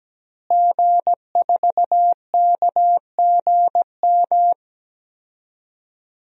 Morse code, heard as a keyed tone, about 17 wpm, 700 Hz. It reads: G4KGM
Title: DXCC-100 Call Signs - 5 Characters - Repeated 2x - 17wpm